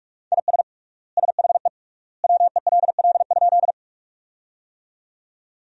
Morse code, heard as a keyed tone, about 46 wpm, 700 Hz. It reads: IS SHE WELL?
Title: 3x Word-Spacing - Sentences from Top 200 Words - 45wpm